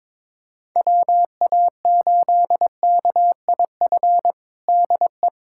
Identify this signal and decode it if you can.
Morse code, heard as a keyed tone, about 22 wpm, 700 Hz. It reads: WA8KIF DE
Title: USA Callsign DE Callsign - 6 Characters - 22wpm